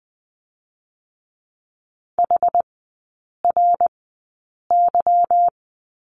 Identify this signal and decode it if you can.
Morse code, heard as a keyed tone, about 20 wpm, spaced as for 8 wpm, 700 Hz. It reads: HRY
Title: 178 Lesson 29e - Slash - Three Characters 20wpm